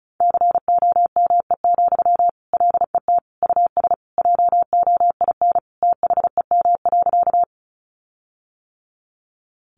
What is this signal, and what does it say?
Morse code, heard as a keyed tone, about 34 wpm, 700 Hz. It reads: COME, LET US JOIN THEM.
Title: Sentences from Top 1400 Words 35wpm